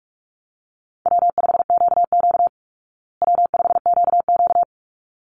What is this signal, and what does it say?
Morse code, heard as a keyed tone, about 44 wpm, 700 Hz. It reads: W5QQ W5QQ
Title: Call Signs Repeated 3x - 45wpm